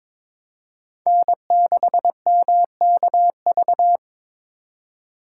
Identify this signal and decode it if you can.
Morse code, heard as a keyed tone, about 26 wpm, 700 Hz. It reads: N6MKV
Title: Speed Racing - USA 1x3 Call Signs 22wpm